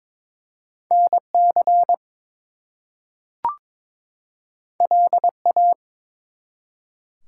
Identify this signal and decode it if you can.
Morse code, heard as a keyed tone, about 22 wpm, 700 Hz. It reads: NC LA
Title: US States and Canadian Provinces 22wpm